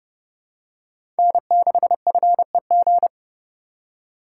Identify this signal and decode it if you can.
Morse code, heard as a keyed tone, about 30 wpm, 700 Hz. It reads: N6FEG
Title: USA Call Signs - 4x Character Spacing - 30wpm